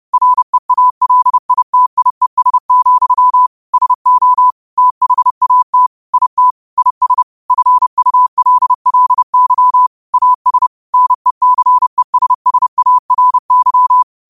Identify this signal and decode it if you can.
Morse code, heard as a keyed tone, about 30 wpm, 1000 Hz. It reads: REALITIES, SO THAT IT IS FULLY AS NECESSARY